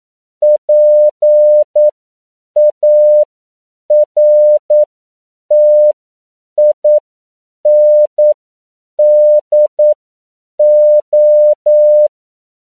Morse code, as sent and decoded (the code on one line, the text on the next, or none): .--. .- .-. - .. -. -.. ---
PARTINDO